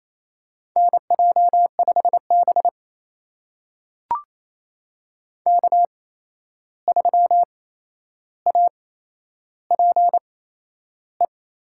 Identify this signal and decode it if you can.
Morse code, heard as a keyed tone, about 27 wpm, spaced as for 11 wpm, 700 Hz. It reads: NJ5B K3APE
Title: USA Call Signs - 8x Character Spacing - 28wpm